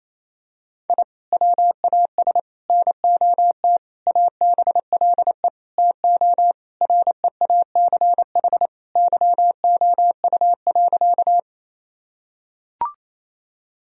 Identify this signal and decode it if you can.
Morse code, heard as a keyed tone, about 28 wpm, 700 Hz. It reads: I WAS NOT ABLE TO REACH YOU.
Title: Sentences from Top 1000 Words - Review Part 2 - 28wpm